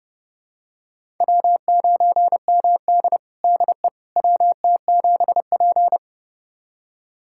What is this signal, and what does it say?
Morse code, heard as a keyed tone, about 30 wpm, 700 Hz. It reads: W9MD DE WT7P
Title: USA Callsign DE Callsign - 4 Characters - 30wpm